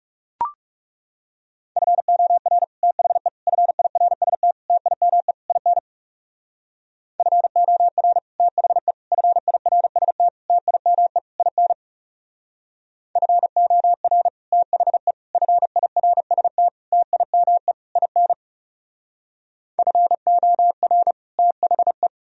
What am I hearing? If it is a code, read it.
Morse code, heard as a keyed tone, about 35 wpm, 700 Hz. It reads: FOR THE FIRST TIME IN FOR THE FIRST TIME IN FOR THE FIRST TIME IN FOR THE
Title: Speed-Racing - Sets of 5 Words 30wpm